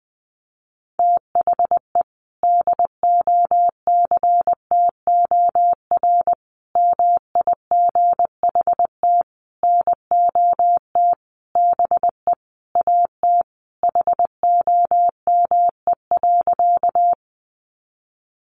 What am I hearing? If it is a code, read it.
Morse code, heard as a keyed tone, about 20 wpm, 700 Hz. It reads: THE DOCTOR MIGHT NOT BE AT HOME.
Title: Sentences from Top 1500 Words 20wpm